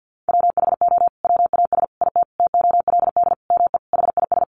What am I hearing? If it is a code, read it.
Morse code, heard as a keyed tone, about 50 wpm, 700 Hz. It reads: WHO WAS IT TOLD ME HIS
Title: Sentences from Top 400 Words 50wpm